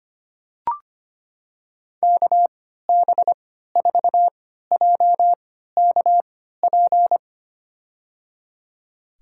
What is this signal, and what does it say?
Morse code, heard as a keyed tone, about 25 wpm, spaced as for 14 wpm, 700 Hz. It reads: KB4JKP
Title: USA Call Signs - 6 Characters - 3x Character Spacing - 25wpm